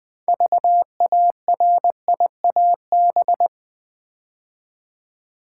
Morse code, heard as a keyed tone, about 20 wpm, 700 Hz. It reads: VARIAB
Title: Build A Word - 3 Letters At A Time - 8-Letter Words - 20wpm